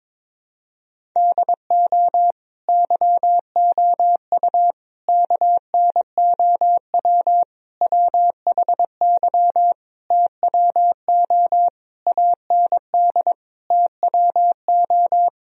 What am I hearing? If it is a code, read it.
Morse code, heard as a keyed tone, about 22 wpm, 700 Hz. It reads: DO YOU KNOW WHY TWO AND TWO
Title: Sentences from Top 1000 Words - Review Part 1 - 22wpm